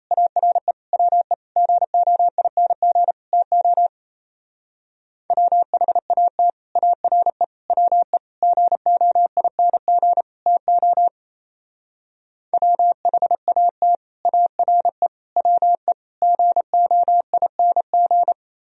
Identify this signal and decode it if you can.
Morse code, heard as a keyed tone, about 34 wpm, 700 Hz. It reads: ARE WE GOING TO WHAT ARE WE GOING TO WHAT ARE WE GOING
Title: Speed-Racing - Sets of 5 Words 28wpm